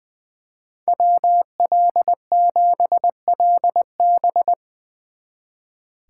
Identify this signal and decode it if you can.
Morse code, heard as a keyed tone, about 22 wpm, 700 Hz. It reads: WL7LB
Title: Speed Racing - USA 2x2 Call Signs 20wpm